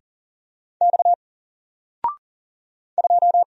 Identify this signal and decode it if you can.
Morse code, heard as a keyed tone, about 40 wpm, 700 Hz. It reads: X 2
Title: Single Character (Letter:Number:Punctuation:Prosign) 40wpm